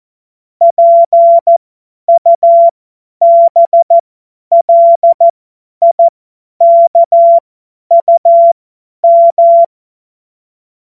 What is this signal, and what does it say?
Morse code, heard as a keyed tone, about 14 wpm, spaced as for 10 wpm, 680 Hz. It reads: PUBLIKUM
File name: Morsekurs / Witze / Witz 04. Oktober 2022 50 F70